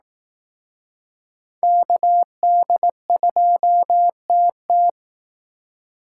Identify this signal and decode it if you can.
Morse code, heard as a keyed tone, about 19 wpm, 700 Hz. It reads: KD2TT
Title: Speed Racing - USA Call Signs 15wpm